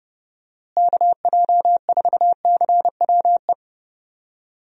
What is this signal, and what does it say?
Morse code, heard as a keyed tone, about 30 wpm, 700 Hz. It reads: KJ4CWE
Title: USA Call Signs - 6 Characters - Repeated 2x - 30wpm